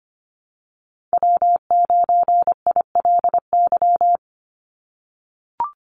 Morse code, heard as a keyed tone, about 25 wpm, 700 Hz. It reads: W9ILY
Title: Call Signs - 25wpm